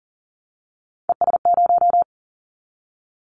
Morse code, heard as a keyed tone, about 40 wpm, 700 Hz. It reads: ES0
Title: Callsign Prefixes for Top 100 DXCC - 40wpm